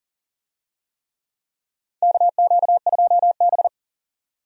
Morse code, heard as a keyed tone, about 40 wpm, 700 Hz. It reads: KQ2B
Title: USA Call Signs - 4 Characters - 40wpm